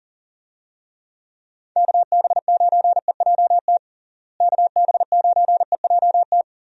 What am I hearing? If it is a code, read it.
Morse code, heard as a keyed tone, about 40 wpm, 700 Hz. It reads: KB9EJT KB9EJT
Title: USA Call Signs - 6 Characters - Repeated 3x - 40wpm